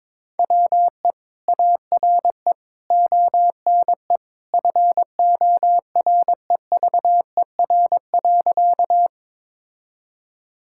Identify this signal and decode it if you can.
Morse code, heard as a keyed tone, about 22 wpm, 700 Hz. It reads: WE ARE ONE FOREVER.
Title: Sentences from Top 1600 Words 22wpm